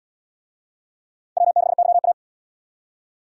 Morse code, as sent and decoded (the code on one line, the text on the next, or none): .-- -.... -.-- .-
W6YA